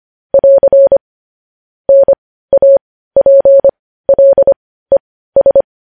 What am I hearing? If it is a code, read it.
Morse code, heard as a keyed tone, about 25 wpm, spaced as for 15 wpm, 550 Hz. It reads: <AR> NAPLES
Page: news-headlines-in-morse-code-at-15-wpm